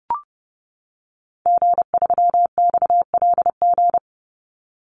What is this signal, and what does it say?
Morse code, heard as a keyed tone, about 30 wpm, 700 Hz. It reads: G3XLG
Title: Call Signs - 30wpm